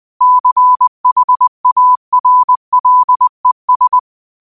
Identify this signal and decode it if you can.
Morse code, heard as a keyed tone, about 20 wpm, 1000 Hz. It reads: CHARLES